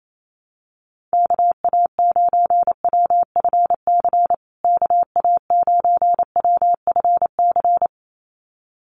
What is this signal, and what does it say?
Morse code, heard as a keyed tone, about 28 wpm, 700 Hz. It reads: KA9WFC KA9WFC
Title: USA Call Signs Repeated 2x 28wpm